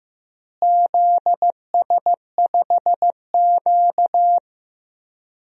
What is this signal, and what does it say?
Morse code, heard as a keyed tone, about 15 wpm, 700 Hz. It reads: ZS5Q
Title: DXCC-100 Call Signs - 4 Characters - Repeated 2x - 15wpm